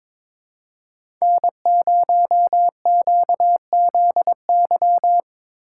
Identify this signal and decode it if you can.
Morse code, heard as a keyed tone, about 22 wpm, 700 Hz. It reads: N0QZY
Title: USA Call Signs - 5 Characters - 3x Character Spacing - 22wpm